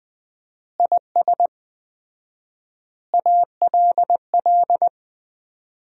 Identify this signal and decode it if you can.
Morse code, heard as a keyed tone, about 20 wpm, 700 Hz. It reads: IS ALL
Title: 4x Word-Spacing - Sentences from Top 100 Words - 20wpm